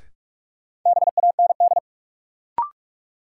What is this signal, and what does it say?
Morse code, heard as a keyed tone, about 46 wpm, 700 Hz. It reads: BAND
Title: Sentences from Top 1600 Words - New Word Drill Alt - 45wpm